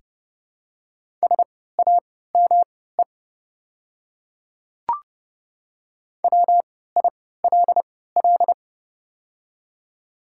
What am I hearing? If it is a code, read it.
Morse code, heard as a keyed tone, about 30 wpm, spaced as for 17 wpm, 700 Hz. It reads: SAME WILL
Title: 3x Character Spacing - Top 300 Words - 30wpm